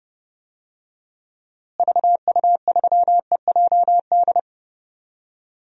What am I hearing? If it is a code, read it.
Morse code, heard as a keyed tone, about 30 wpm, 700 Hz. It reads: VU3EJD
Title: DXCC-100 Call Signs - 6 Characters - Repeated 3x - 30wpm